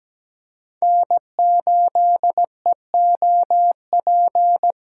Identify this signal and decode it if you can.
Morse code, heard as a keyed tone, about 17 wpm, 700 Hz. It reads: N8EOP
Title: USA Call Signs - 5 Characters - 3x Character Spacing - 17wpm